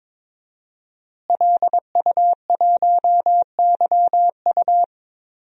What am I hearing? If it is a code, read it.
Morse code, heard as a keyed tone, about 22 wpm, 700 Hz. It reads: LU1YU
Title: DXCC-100 Call Signs - 5 Characters - Repeated 2x - 22wpm